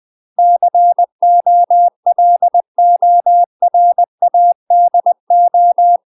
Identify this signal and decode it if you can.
Morse code, heard as a keyed tone, about 20 wpm, 700 Hz. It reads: COLORADO
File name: UK MASTER QSOs TRAINING / 20WPM / UK MASTER QSOs 0073